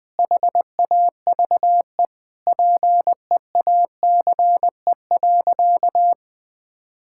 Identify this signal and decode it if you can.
Morse code, heard as a keyed tone, about 21 wpm, 700 Hz. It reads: HAVE PEACE.
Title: Sentences with 9-Word Phrases - 20wpm